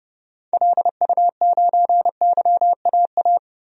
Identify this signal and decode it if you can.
Morse code, heard as a keyed tone, about 30 wpm, 700 Hz. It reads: LU9YAA
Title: DXCC-100 Call Signs - 6 Characters - 30wpm